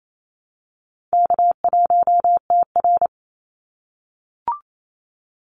Morse code, heard as a keyed tone, about 28 wpm, 700 Hz. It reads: K1TR
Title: Easy Call Signs - 28wpm